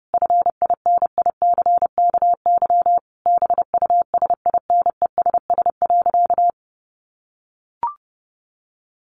Morse code, heard as a keyed tone, about 28 wpm, 700 Hz. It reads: FINICKY BUSINESS.
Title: A Sound Of Thunder 30wpm